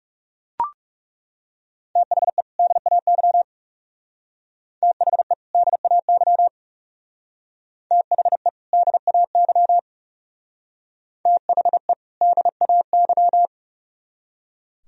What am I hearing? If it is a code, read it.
Morse code, heard as a keyed tone, about 37 wpm, 700 Hz. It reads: THE DAY THE DAY THE DAY THE DAY
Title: Speed-Racing - Sets of 2 Words 30wpm